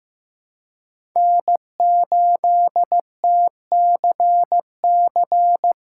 Morse code, heard as a keyed tone, about 15 wpm, 700 Hz. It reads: N8TCC
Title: USA Callsign DE Callsign - 5 Characters - 15wpm